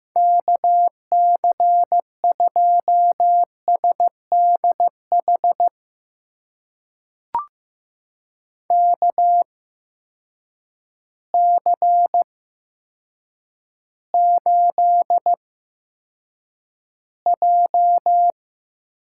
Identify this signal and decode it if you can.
Morse code, heard as a keyed tone, about 15 wpm, spaced as for 8 wpm, 700 Hz. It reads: KC2SDH KC8J
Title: USA Call Signs - 8x Character Spacing - 15wpm